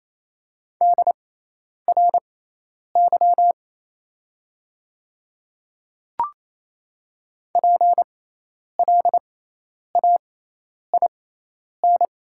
Morse code, heard as a keyed tone, about 28 wpm, spaced as for 10 wpm, 700 Hz. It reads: DRY PLAIN
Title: 6x Character Spacing - Top 500 Words - 28wpm